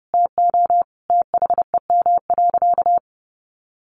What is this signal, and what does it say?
Morse code, heard as a keyed tone, about 28 wpm, 700 Hz. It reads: TO THEM.
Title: Sentences from Top 1200 Words 30wpm